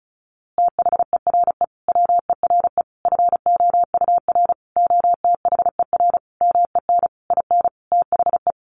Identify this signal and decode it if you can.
Morse code, heard as a keyed tone, about 34 wpm, 700 Hz. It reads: THERE WERE FOUR OTHER MEN IN THE
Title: A Sound Of Thunder 35wpm